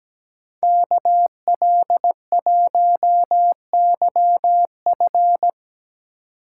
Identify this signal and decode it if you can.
Morse code, heard as a keyed tone, about 18 wpm, 700 Hz. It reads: KL1YF
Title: Speed Racing - USA Call Signs 17wpm